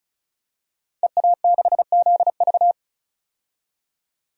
Morse code, heard as a keyed tone, about 36 wpm, 700 Hz. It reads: EA6ZV
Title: DXCC-100 Call Signs - 5 Characters - 35wpm